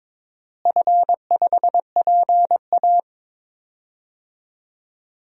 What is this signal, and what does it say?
Morse code, heard as a keyed tone, about 22 wpm, 700 Hz. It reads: F5PA
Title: DXCC-100 Call Signs - 4 Characters - Repeated 2x - 22wpm